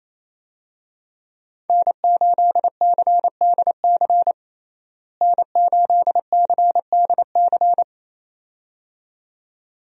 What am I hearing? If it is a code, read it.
Morse code, heard as a keyed tone, about 28 wpm, 700 Hz. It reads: N8CDC N8CDC
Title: USA Call Signs - 5 Characters - Repeated 2x - 28wpm